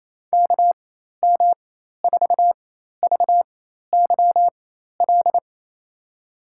Morse code, heard as a keyed tone, about 28 wpm, spaced as for 13 wpm, 700 Hz. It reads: KM4VYL
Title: USA Call Signs - 4x Character Spacing - 28wpm